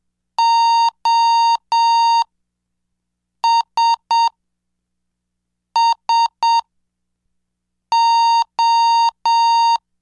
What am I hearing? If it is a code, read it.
Morse code, heard as a keyed tone, about 7 wpm, spaced as for 4 wpm, 920 Hz. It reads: OSSO